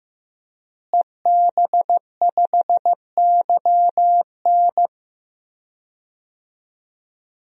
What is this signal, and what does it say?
Morse code, heard as a keyed tone, about 15 wpm, 700 Hz. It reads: EB5YN
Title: DXCC-100 Call Signs - 5 Characters - Repeated 2x - 15wpm